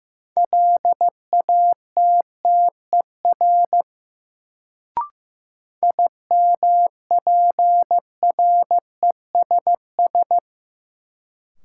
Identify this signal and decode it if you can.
Morse code, heard as a keyed tone, about 15 wpm, 700 Hz. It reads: LATTER IMPRESS
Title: Sentences from Top 1600 Words - New Word Drill Alt - 15wpm